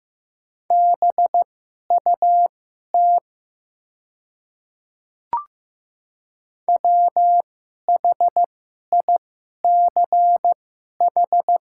Morse code, heard as a keyed tone, about 15 wpm, spaced as for 11 wpm, 700 Hz. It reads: BUT WHICH
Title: 2x Character Spacing - Top 300 Words - 15wpm